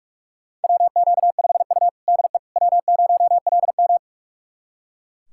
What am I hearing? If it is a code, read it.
Morse code, heard as a keyed tone, about 41 wpm, 700 Hz. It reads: WQ5U DE W0LM
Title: USA Callsign DE Callsign - 4 Characters - 45wpm